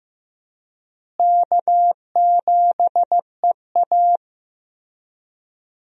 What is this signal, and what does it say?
Morse code, heard as a keyed tone, about 15 wpm, 700 Hz. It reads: K7EA
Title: USA Call Signs - 4 Characters - Repeated 3x - 15wpm